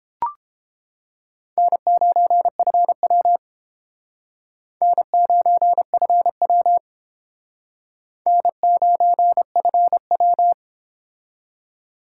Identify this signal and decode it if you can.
Morse code, heard as a keyed tone, about 29 wpm, 700 Hz. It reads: N9FW N9FW N9FW